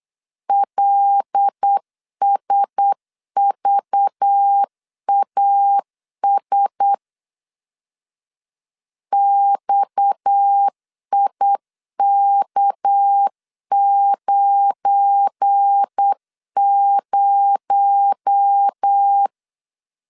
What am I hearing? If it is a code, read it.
Morse code, heard as a keyed tone, about 8 wpm, 800 Hz. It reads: LSVAS XIK90
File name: LEZIONI / 8 YJ45 / 8 YJ 8wpm 3313